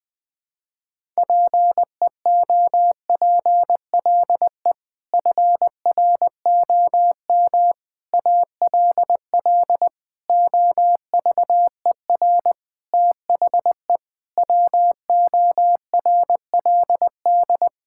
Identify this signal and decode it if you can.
Morse code, heard as a keyed tone, about 20 wpm, 700 Hz. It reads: PEOPLE FROM ALL OVER THE WORLD
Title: Sentences with 6-Word Phrases - 20wpm